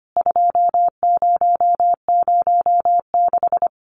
Morse code, heard as a keyed tone, about 25 wpm, 700 Hz. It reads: 2006
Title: CWT Exchange 25wpm